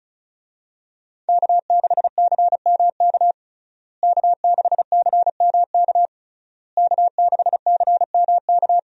Morse code, heard as a keyed tone, about 36 wpm, 700 Hz. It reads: K6CMK K6CMK K6CMK
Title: USA Call Signs - 5 Characters - Repeated 3x - 35wpm